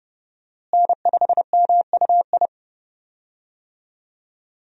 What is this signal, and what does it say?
Morse code, heard as a keyed tone, about 30 wpm, 700 Hz. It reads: N5MUI
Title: USA Call Signs - 5 Characters - Repeated 2x - 30wpm